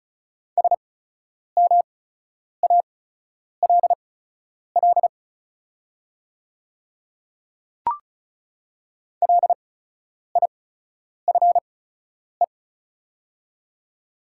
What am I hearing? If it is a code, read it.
Morse code, heard as a keyed tone, about 36 wpm, spaced as for 10 wpm, 700 Hz. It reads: SMALL LIFE
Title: 8x Character Spacing - Top 300 Words - 35wpm